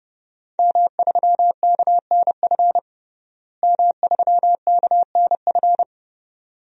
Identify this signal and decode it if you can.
Morse code, heard as a keyed tone, about 30 wpm, 700 Hz. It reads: M3KNF M3KNF
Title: DXCC-100 Call Signs - 5 Characters - Repeated 2x - 30wpm